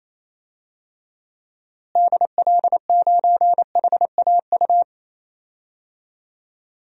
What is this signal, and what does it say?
Morse code, heard as a keyed tone, about 28 wpm, 700 Hz. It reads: DL9HAU
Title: DXCC-100 Call Signs - 6 Characters - Repeated 3x - 28wpm